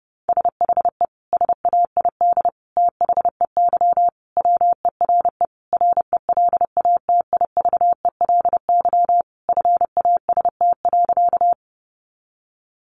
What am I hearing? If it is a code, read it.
Morse code, heard as a keyed tone, about 28 wpm, 700 Hz. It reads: SHE SAID THEY WERE RELATIVELY FAST.
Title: Sentences from Top 1100 Words 30wpm